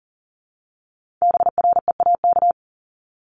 Morse code, heard as a keyed tone, about 39 wpm, 700 Hz. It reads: BREAK
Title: Multi-letter Phonemes 40wpm